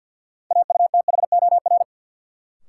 Morse code, heard as a keyed tone, about 51 wpm, 700 Hz. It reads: AUTHOR
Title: Sentences from Top 1200 Words - New Word Drill - 50wpm